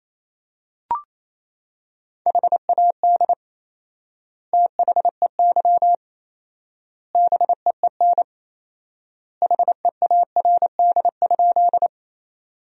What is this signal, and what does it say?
Morse code, heard as a keyed tone, about 28 wpm, 700 Hz. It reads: HAD THEY BEEN HEARD?
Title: 4x Word-Spacing - Sentences from Top 400 Words - 28wpm